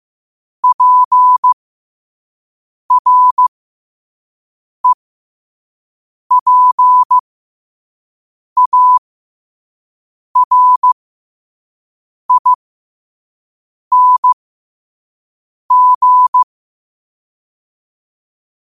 Morse code, heard as a keyed tone, about 15 wpm, spaced as for 5 wpm, 1000 Hz. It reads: PREPARING
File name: qotdfiles / 5wpm